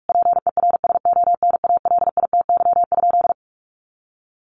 Morse code, heard as a keyed tone, about 44 wpm, 700 Hz. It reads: PERSONALITY?
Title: Sentences from Top 1400 Words 45wpm